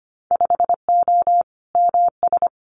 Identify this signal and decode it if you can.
Morse code, heard as a keyed tone, about 25 wpm, 700 Hz. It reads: 5O MS
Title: Winter Field Day Exchanges Encore 25wpm